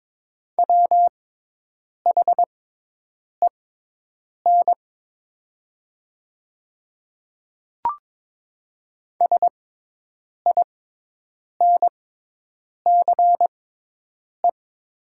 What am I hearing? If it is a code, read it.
Morse code, heard as a keyed tone, about 22 wpm, spaced as for 8 wpm, 700 Hz. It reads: WHEN SINCE
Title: 6x Character Spacing - Top 500 Words - 22wpm